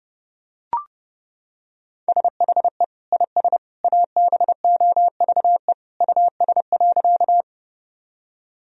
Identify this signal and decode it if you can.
Morse code, heard as a keyed tone, about 30 wpm, 700 Hz. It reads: SHE IS ABOVE US.
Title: Sentences from Top 1000 Words - Review Part 2 - 30wpm